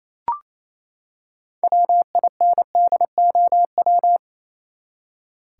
Morse code, heard as a keyed tone, about 27 wpm, 700 Hz. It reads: WINDOW
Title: Six Letter Words - 28wpm